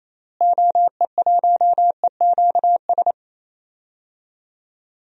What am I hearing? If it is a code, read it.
Morse code, heard as a keyed tone, about 28 wpm, 700 Hz. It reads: OE1EQS
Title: DXCC-100 Call Signs - 6 Characters - Repeated 3x - 28wpm